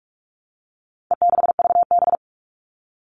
Morse code, heard as a keyed tone, about 44 wpm, 700 Hz. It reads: E6VB
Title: All Cut Numbers - 4-Digit 45wpm